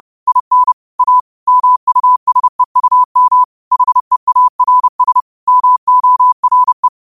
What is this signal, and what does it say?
Morse code, heard as a keyed tone, about 30 wpm, 1000 Hz. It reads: IN A MUSEUM HEARS MORE
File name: qotdfiles / 30wpm